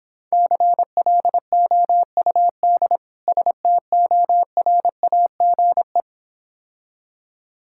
Morse code, heard as a keyed tone, about 26 wpm, 700 Hz. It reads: CLOUD STORAGE